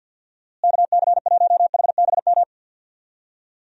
Morse code, heard as a keyed tone, about 50 wpm, 700 Hz. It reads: KC1HBM
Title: USA Call Signs - 6 Characters - Repeated 3x - 50wpm